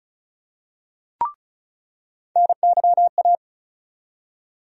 Morse code, heard as a keyed tone, about 34 wpm, 700 Hz. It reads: NYA
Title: USA Call Sign Postfixes - 3 Letters 35wpm